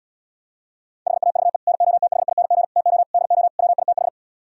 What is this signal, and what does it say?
Morse code, heard as a keyed tone, about 77 wpm, 700 Hz. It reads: HAVE NOTHING TO DO WITH